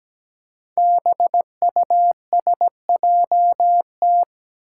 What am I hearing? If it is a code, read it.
Morse code, heard as a keyed tone, about 17 wpm, 700 Hz. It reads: BUSJT
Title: Five Challenging Letter-Number Combinations 17wpm